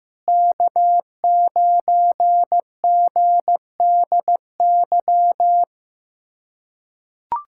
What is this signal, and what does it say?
Morse code, heard as a keyed tone, about 15 wpm, 700 Hz. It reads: K9GDY
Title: USA Callsign DE Callsign 15wpm